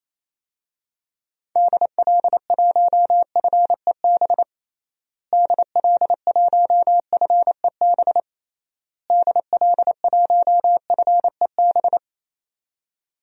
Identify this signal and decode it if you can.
Morse code, heard as a keyed tone, about 27 wpm, 700 Hz. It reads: DL1FEB DL1FEB DL1FEB
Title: DXCC-100 Call Signs - 6 Characters - Repeated 3x - 28wpm